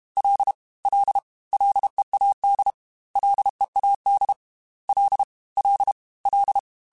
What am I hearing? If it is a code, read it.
Morse code, heard as a keyed tone, about 32 wpm, 800 Hz. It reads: L L LEAD LEAD L L L
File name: F and L / F and L mp3 / F and L Prefixes 3x2 Twice @30